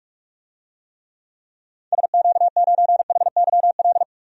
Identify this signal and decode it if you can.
Morse code, heard as a keyed tone, about 41 wpm, 700 Hz. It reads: SQ9HYL
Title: DXCC-100 Call Signs - 6 Characters - 45wpm